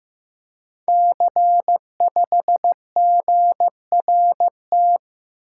Morse code, heard as a keyed tone, about 15 wpm, 700 Hz. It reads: C5GRT
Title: Code Groups - 5 Letter:Numbers - 15wpm